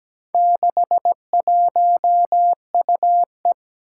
Morse code, heard as a keyed tone, about 17 wpm, 700 Hz. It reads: 61UE
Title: Four Challenging Letter-Number Combinations 17wpm